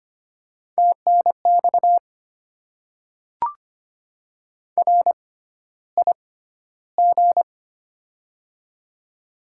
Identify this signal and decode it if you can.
Morse code, heard as a keyed tone, about 25 wpm, spaced as for 13 wpm, 700 Hz. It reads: TNX RIG
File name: Top 20 QSO Elements - 6x Character Spacing - 25wpm